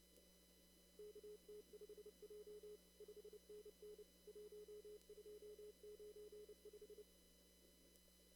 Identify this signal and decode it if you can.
Morse code, heard as a keyed tone, about 30 wpm, 420 Hz. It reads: KT5J 5NN 1295
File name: CW / kt5j wpx cw-op-k5pi